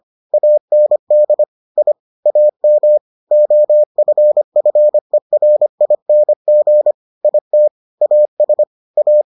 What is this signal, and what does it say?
Morse code, heard as a keyed tone, about 25 wpm, 600 Hz. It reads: AND I AM OFFERING IT AS A